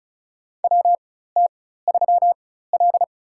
Morse code, heard as a keyed tone, about 36 wpm, spaced as for 16 wpm, 700 Hz. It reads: WT3L
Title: USA Call Signs - 4 Characters - 4x Character Spacing - 35wpm